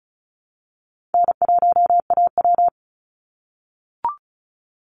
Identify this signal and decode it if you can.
Morse code, heard as a keyed tone, about 34 wpm, 700 Hz. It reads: N1AW
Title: NA SOTA Chaser Callsigns 35wpm